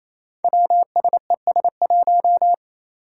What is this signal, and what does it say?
Morse code, heard as a keyed tone, about 28 wpm, 700 Hz. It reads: WSES1
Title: Five Challenging Letter-Number Combinations 28wpm